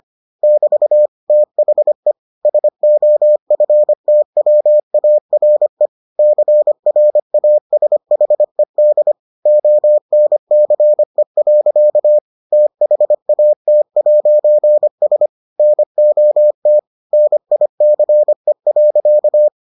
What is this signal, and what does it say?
Morse code, heard as a keyed tone, about 25 wpm, 600 Hz. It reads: <BT> THE SOFTWARE CRASHED ONCE. THAT'S NOT NICE.